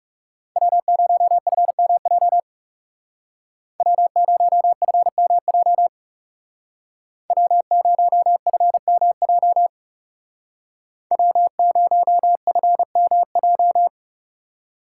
Morse code, heard as a keyed tone, about 37 wpm, 700 Hz. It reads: W0FMJ W0FMJ W0FMJ W0FMJ
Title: Speed Racing - USA 1x3 Call Signs 30wpm